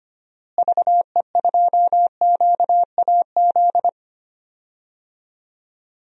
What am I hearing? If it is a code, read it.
Morse code, heard as a keyed tone, about 25 wpm, 700 Hz. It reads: VE2QAZ
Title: DXCC-100 Call Signs - 6 Characters - Repeated 2x - 25wpm